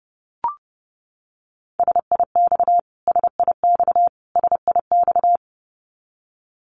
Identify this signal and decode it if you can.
Morse code, heard as a keyed tone, about 30 wpm, 700 Hz. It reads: SIX SIX SIX
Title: Top 500 Words Repeated 30wpm